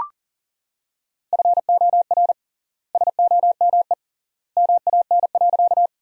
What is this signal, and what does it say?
Morse code, heard as a keyed tone, about 40 wpm, 700 Hz. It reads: FOR SOME MAN.
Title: 3x Word-Spacing - Sentences from Top 200 Words - 40wpm